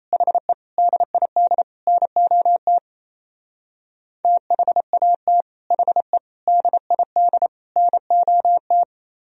Morse code, heard as a keyed tone, about 33 wpm, 700 Hz. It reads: HE DID NOT THAT HE DID NOT
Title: Speed-Racing - Sets of 4 Words 28wpm